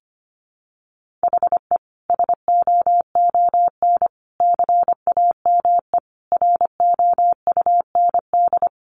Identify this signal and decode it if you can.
Morse code, heard as a keyed tone, about 25 wpm, 700 Hz. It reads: HE SOON CAME ROUND
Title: Sentences from Top 600 Words 25wpm